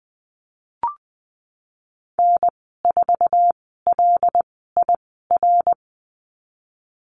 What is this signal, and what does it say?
Morse code, heard as a keyed tone, about 20 wpm, spaced as for 15 wpm, 700 Hz. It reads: N4LIR
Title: 088 Lesson 15g - Number 5 - Call Signs 20wpm